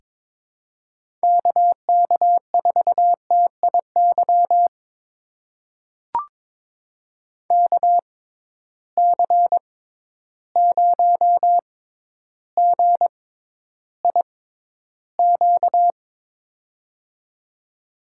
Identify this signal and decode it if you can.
Morse code, heard as a keyed tone, about 22 wpm, spaced as for 11 wpm, 700 Hz. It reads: KK4TIY KC0GIQ
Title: USA Call Signs - 6 Characters - 6x Character Spacing - 22wpm